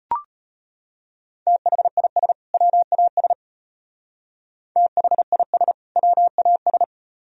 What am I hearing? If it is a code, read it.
Morse code, heard as a keyed tone, about 33 wpm, 700 Hz. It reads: THIS WAS THIS WAS